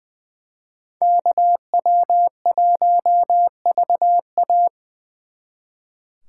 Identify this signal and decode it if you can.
Morse code, heard as a keyed tone, about 24 wpm, 700 Hz. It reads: KW1VA
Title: Speed Racing - USA 2x2 Call Signs 20wpm